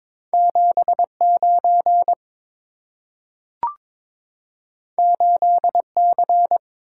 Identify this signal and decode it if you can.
Morse code, heard as a keyed tone, about 22 wpm, 690 Hz. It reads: 7 9 8 C
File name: practice / 22wpm / Two Letter-Number Combinations - 22wpm